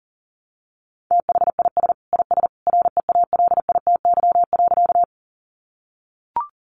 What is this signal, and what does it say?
Morse code, heard as a keyed tone, about 40 wpm, 700 Hz. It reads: THIS IS REALITY.
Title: Sentences from Top 1200 Words 40wpm